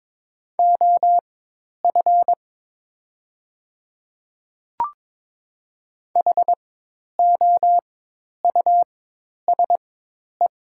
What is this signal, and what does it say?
Morse code, heard as a keyed tone, about 22 wpm, spaced as for 10 wpm, 700 Hz. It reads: OF HOUSE
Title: 4x Character Spacing - Top 200 Words - 22wpm